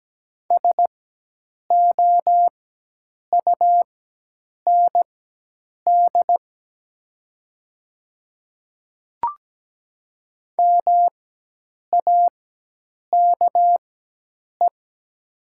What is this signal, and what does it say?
Morse code, heard as a keyed tone, about 17 wpm, spaced as for 8 wpm, 700 Hz. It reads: SOUND MAKE
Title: 4x Character Spacing - Top 200 Words - 17wpm